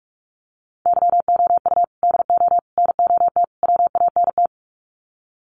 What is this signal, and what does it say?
Morse code, heard as a keyed tone, about 44 wpm, 700 Hz. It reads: YOU DO NOT WANT
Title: Sets of 4 Words 45wpm